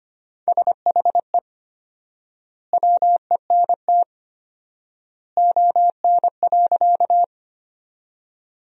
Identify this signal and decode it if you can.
Morse code, heard as a keyed tone, about 25 wpm, 700 Hz. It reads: SHE WENT ON.
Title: 4x Word-Spacing - Sentences from Top 200 Words - 25wpm